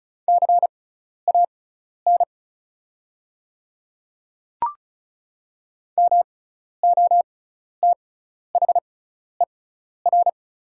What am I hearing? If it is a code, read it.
Morse code, heard as a keyed tone, about 36 wpm, spaced as for 12 wpm, 700 Hz. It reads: CAN MOTHER
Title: 6x Character Spacing - Top 200 Words - 35wpm